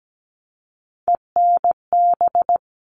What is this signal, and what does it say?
Morse code, heard as a keyed tone, about 17 wpm, 700 Hz. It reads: ENB
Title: All Cut Numbers - 3-Digit 17wpm